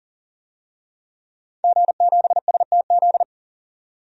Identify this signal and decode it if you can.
Morse code, heard as a keyed tone, about 40 wpm, 700 Hz. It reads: G7STZ
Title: DXCC-100 Call Signs - 5 Characters - 40wpm